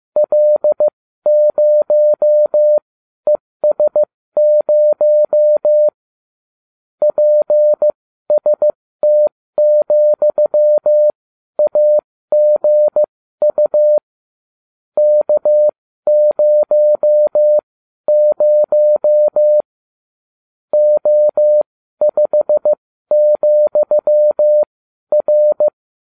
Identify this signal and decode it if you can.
Morse code, heard as a keyed tone, about 15 wpm, spaced as for 12 wpm, 600 Hz. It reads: L0ES0 PST,AGU K00 O5,R